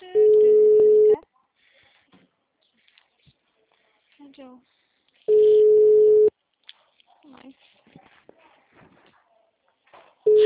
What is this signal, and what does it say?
Ringback tone (Europe, 425 Hz), with speech.
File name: out- -2006- - - .50028